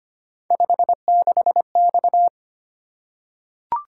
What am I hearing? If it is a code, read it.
Morse code, heard as a keyed tone, about 25 wpm, 700 Hz. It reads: 56X
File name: Three Letter-Number Combinations - 25wpm